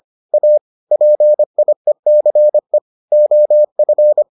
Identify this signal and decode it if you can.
Morse code, heard as a keyed tone, about 25 wpm, 600 Hz. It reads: A PIECE OF